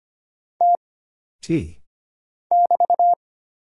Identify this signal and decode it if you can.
Morse code, heard as a keyed tone, about 25 wpm, 700 Hz. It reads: T <BT>
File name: Single Character (Letter:Number:Punctuation:Prosign) - Mind-Melt - 25wpm